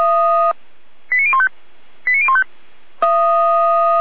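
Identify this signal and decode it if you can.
Five-tone selective call (ZVEI), heard as quick runs of five tones, tones of about 70 ms. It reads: 89015 89015
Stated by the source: ZVEI five-tone selective call, code 89068, received on an SDR.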